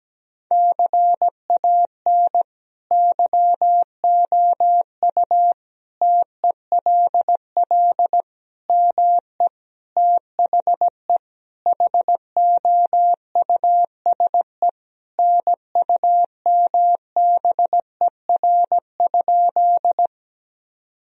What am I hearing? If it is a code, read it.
Morse code, heard as a keyed tone, about 17 wpm, 700 Hz. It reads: CAN YOU TELL ME THE HOUSE NUMBER?
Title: Sentences from Top 1000 Words - Review Part 1 - 17wpm